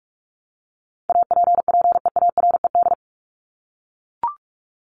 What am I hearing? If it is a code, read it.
Morse code, heard as a keyed tone, about 44 wpm, 700 Hz. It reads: APPEARED
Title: ED Suffix - Top 100 Words with Base Words 45wpm